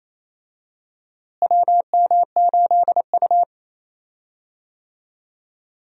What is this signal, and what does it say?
Morse code, heard as a keyed tone, about 28 wpm, 700 Hz. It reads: WM8U
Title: USA Call Signs - 4 Characters - Repeated 3x - 28wpm